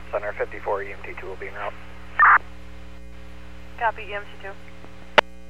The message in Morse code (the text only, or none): none